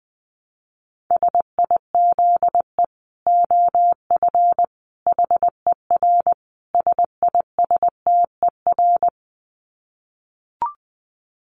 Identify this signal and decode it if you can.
Morse code, heard as a keyed tone, about 20 wpm, 700 Hz. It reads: SIZE OF HER SISTER
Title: Dit Challenge 20wpm